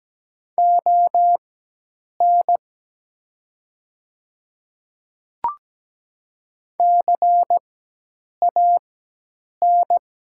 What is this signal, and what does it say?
Morse code, heard as a keyed tone, about 17 wpm, spaced as for 8 wpm, 700 Hz. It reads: ON CAN
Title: 4x Character Spacing - Top 100 Words - 17wpm